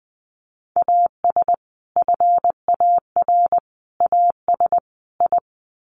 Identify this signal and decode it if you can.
Morse code, heard as a keyed tone, about 20 wpm, 700 Hz. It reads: AS FAR AS I
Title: Sets of 4 Words 20wpm